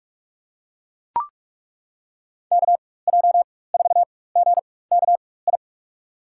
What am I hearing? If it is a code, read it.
Morse code, heard as a keyed tone, about 42 wpm, spaced as for 21 wpm, 700 Hz. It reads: KJ4GKI
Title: USA Call Signs - 6 Characters - 4x Character Spacing - 45wpm